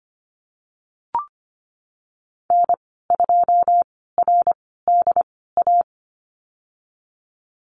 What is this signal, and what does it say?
Morse code, heard as a keyed tone, about 25 wpm, spaced as for 16 wpm, 700 Hz. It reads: N2RDA
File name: Lesson 15g - Number 5 - Call Signs 25wpm